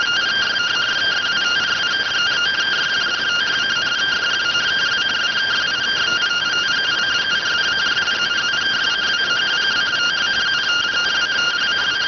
Two-tone teleprinter signal, about 170 Hz shift.